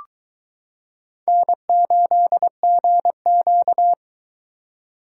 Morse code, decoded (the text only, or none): N8GQ